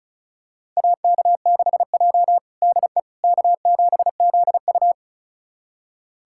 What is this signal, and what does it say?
Morse code, heard as a keyed tone, about 36 wpm, 700 Hz. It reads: AK6J DE K7ZU
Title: USA Callsign DE Callsign - 4 Characters - 35wpm